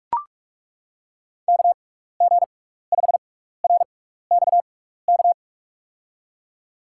Morse code, heard as a keyed tone, about 42 wpm, spaced as for 15 wpm, 700 Hz. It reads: KG5RXK
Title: USA Call Signs - 6 Characters - 6x Character Spacing - 45wpm